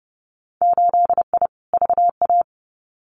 Morse code, heard as a keyed tone, about 30 wpm, 700 Hz. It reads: 8I VA
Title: Winter Field Day Exchanges Encore 30wpm